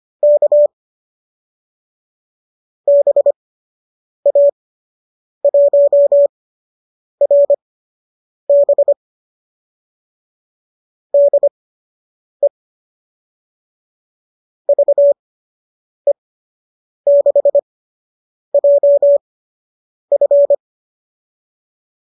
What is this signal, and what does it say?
Morse code, heard as a keyed tone, about 25 wpm, spaced as for 8 wpm, 590 Hz. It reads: K BA1RB DE VE6JF